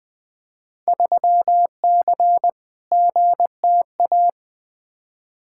Morse code, heard as a keyed tone, about 20 wpm, 700 Hz. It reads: 3C GTA
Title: Field Day Exchanges Encore - 20wpm